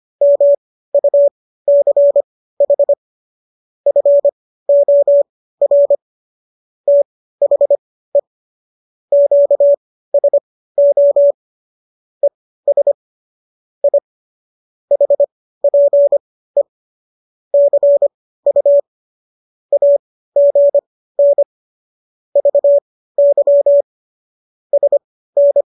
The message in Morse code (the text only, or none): MUCH FOR THE QSO ES I HPE CU AGN VY SN